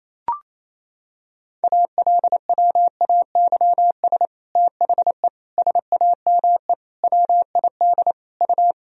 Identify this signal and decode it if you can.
Morse code, heard as a keyed tone, about 28 wpm, 700 Hz. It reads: ALWAYS THE SAME WID U
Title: Sentences from Top 300 Words and Abbreviations 28wpm